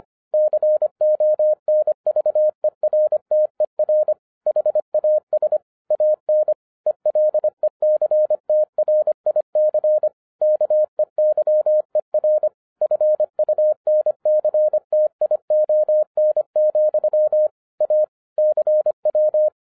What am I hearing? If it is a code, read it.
Morse code, heard as a keyed tone, about 25 wpm, 610 Hz. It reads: CONVERTER HAS AN ELECTRIC KEYER FUNCTION, A CW
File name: 2011-10-mp3 / n09-CW TO PS2 CONVERTER 25wpm